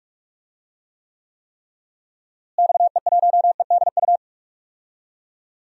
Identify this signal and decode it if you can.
Morse code, heard as a keyed tone, about 42 wpm, 700 Hz. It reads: XE1EDU
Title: DXCC-100 Call Signs - 6 Characters - Repeated 3x - 45wpm